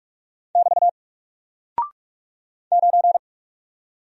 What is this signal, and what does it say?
Morse code, heard as a keyed tone, about 46 wpm, 700 Hz. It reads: <BT> 9
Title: Single Character (Letter:Number:Punctuation:Prosign) Alt 45wpm